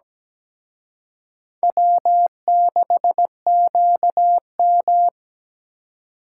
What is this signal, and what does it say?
Morse code, heard as a keyed tone, about 20 wpm, 700 Hz. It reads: W6QM
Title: Speed Racing - USA 1x2 Call Signs 17wpm